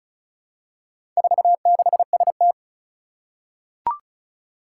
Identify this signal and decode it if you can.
Morse code, heard as a keyed tone, about 36 wpm, 700 Hz. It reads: 46ST
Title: Four Challenging Letter-Number Combinations 35wpm